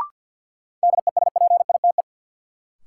Morse code, heard as a keyed tone, about 52 wpm, 700 Hz. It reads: DESPITE
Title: Sets of 4 Words - New Word Drill - 50wpm